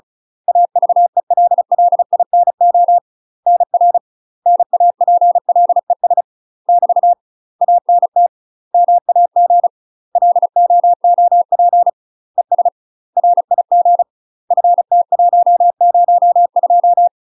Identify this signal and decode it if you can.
Morse code, heard as a keyed tone, about 36 wpm, 700 Hz. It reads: AVELLINO NR NAPLES <BT> ANT MAG LOOP ES RIG FT102
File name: UK MASTER QSOs TRAINING / 35WPM / UK MASTER QSOs 0032